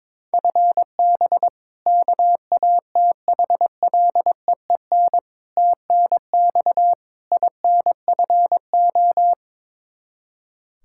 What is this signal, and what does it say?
Morse code, heard as a keyed tone, about 22 wpm, 700 Hz. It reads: FB KATHLEEN TNX INFO
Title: Practice QSO - Exchange 3 - More Info - 22wpm